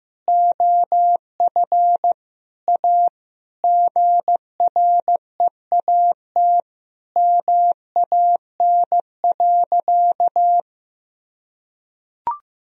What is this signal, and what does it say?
Morse code, heard as a keyed tone, about 15 wpm, 700 Hz. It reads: OF A GREAT MAN.
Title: Sentences from Top 1000 Words - Review Part 2 - 15wpm